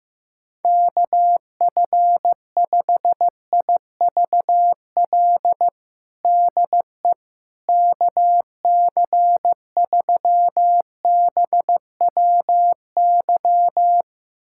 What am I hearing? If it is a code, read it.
Morse code, heard as a keyed tone, about 15 wpm, 700 Hz. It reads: KF5IVL DE KC3BWY
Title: USA Callsign DE Callsign - 6 Characters - 15wpm